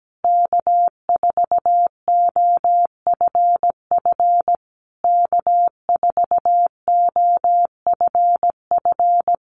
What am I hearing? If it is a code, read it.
Morse code, heard as a keyed tone, about 17 wpm, 700 Hz. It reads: K4OFF K4OFF
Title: Common QSO Phrases Encore 17wpm